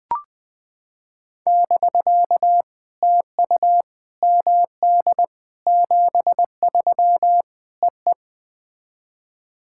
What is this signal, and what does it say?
Morse code, heard as a keyed tone, about 21 wpm, 700 Hz. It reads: <BK> TU MD 73 EE
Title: POTA QSO Practice - 20wpm